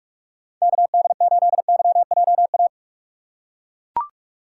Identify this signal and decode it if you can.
Morse code, heard as a keyed tone, about 42 wpm, 700 Hz. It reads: KD8YJA
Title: USA Call Signs - 8x Character Spacing - 45wpm